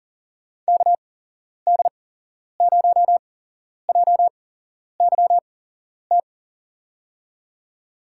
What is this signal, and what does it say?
Morse code, heard as a keyed tone, about 40 wpm, spaced as for 11 wpm, 700 Hz. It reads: KD0JYT
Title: USA Call Signs - 6 Characters - 8x Character Spacing - 40wpm